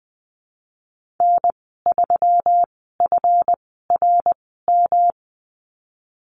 Morse code, heard as a keyed tone, about 20 wpm, spaced as for 15 wpm, 700 Hz. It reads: N3FRM
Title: 161 Lesson 26g - Letter g - Call Signs 20wpm